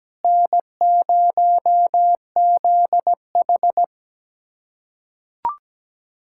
Morse code, heard as a keyed tone, about 17 wpm, 700 Hz. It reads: N0ZH
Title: USA Callsign DE Callsign - 4 Characters - 17wpm